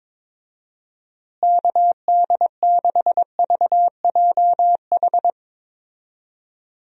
Morse code, heard as a keyed tone, about 22 wpm, 700 Hz. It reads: KD6VJH
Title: USA Call Signs - 6 Characters - Repeated 2x - 22wpm